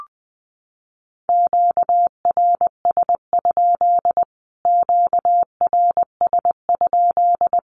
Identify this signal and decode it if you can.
Morse code, heard as a keyed tone, about 19 wpm, 700 Hz. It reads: QRS? QRS?
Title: Q-Codes Repeated 3x 20wpm